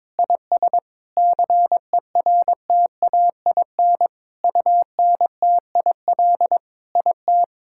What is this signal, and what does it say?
Morse code, heard as a keyed tone, about 21 wpm, 700 Hz. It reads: IS CERTAIN UNTIL IT
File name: practice / 22wpm / Sentences with 8-Word Phrases - 22wpm